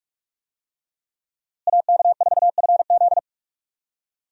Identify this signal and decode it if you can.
Morse code, heard as a keyed tone, about 45 wpm, 700 Hz. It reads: AK4FZ
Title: USA Call Signs - 5 Characters - Repeated 2x - 45wpm